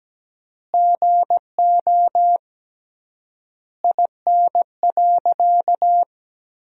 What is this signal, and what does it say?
Morse code, heard as a keyed tone, about 17 wpm, 700 Hz. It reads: GO IN.
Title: 3x Word-Spacing - Sentences from Top 300 Words - 17wpm